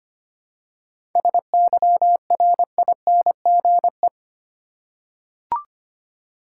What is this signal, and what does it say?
Morse code, heard as a keyed tone, about 25 wpm, 690 Hz. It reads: SYRINGE
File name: practice / 25wpm / Difficult Words with Hints - 25wpm